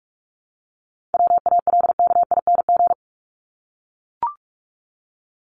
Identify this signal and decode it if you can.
Morse code, heard as a keyed tone, about 44 wpm, 700 Hz. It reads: WALKING
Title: ING Suffix - Top 100 Words with Base Words 45wpm